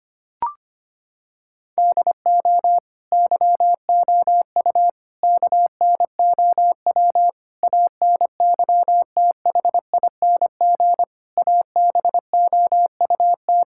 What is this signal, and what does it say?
Morse code, heard as a keyed tone, about 25 wpm, 700 Hz. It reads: DO YOU KNOW ANYTHING ABOUT
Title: Sentences from Top 1600 Words 25wpm